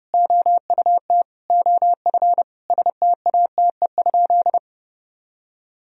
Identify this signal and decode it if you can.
Morse code, heard as a keyed tone, about 30 wpm, 700 Hz. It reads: OUT OF STATE?
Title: Sentences from Top 1000 Words - Review Part 2 - 30wpm